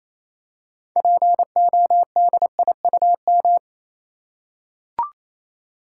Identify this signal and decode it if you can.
Morse code, heard as a keyed tone, about 27 wpm, 690 Hz. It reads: PODIUM
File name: Easy Words - Speed-Racing with Hints - 22wpm